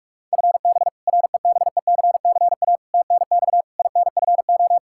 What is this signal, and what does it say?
Morse code, heard as a keyed tone, about 46 wpm, 700 Hz. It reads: FB REBECCA TNX INFO
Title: Practice QSO - Exchange 3 - More Info - 45wpm